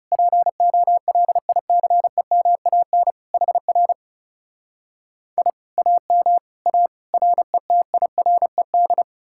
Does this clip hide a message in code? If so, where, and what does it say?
Yes: Morse code from 0.1 to 9.0 s, heard as a keyed tone, about 40 wpm, 700 Hz. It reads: POLICEMAN HR I AM A RETIRED